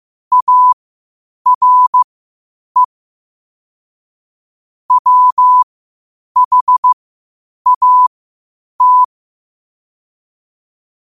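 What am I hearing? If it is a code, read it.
Morse code, heard as a keyed tone, about 15 wpm, spaced as for 8 wpm, 1000 Hz. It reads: ARE WHAT
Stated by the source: Morse code quotes at 8 WPM.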